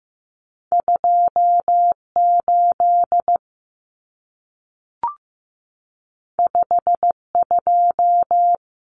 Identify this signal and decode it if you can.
Morse code, heard as a keyed tone, about 15 wpm, 700 Hz. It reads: 2 8 5 2
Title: Numbers - 2-Digit 15wpm